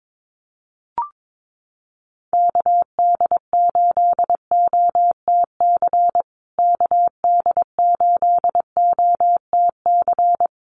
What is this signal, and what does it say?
Morse code, heard as a keyed tone, about 22 wpm, 700 Hz. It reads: KD8OTC KD8OTC
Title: USA Call Signs Repeated 3x 22wpm